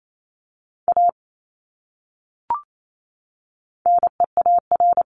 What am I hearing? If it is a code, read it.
Morse code, heard as a keyed tone, about 28 wpm, 700 Hz. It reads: A NEAR
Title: Top 200 Words Repeated 28wpm